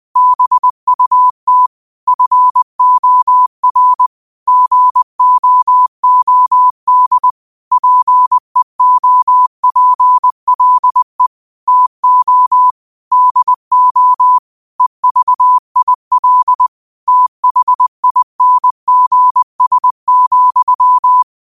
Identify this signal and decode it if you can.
Morse code, heard as a keyed tone, about 20 wpm, 1000 Hz. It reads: BUT FOR GOOD PEOPLE TO DO EVIL THINGS,
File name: qotdfiles / 20wpm